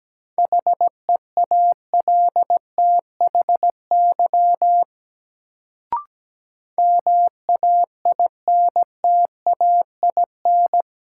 Morse code, heard as a keyed tone, about 17 wpm, 700 Hz. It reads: HEALTHY MAINTAIN
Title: Sentences from Top 1000 Words - Review Part 5 - New Word Drill - 17wpm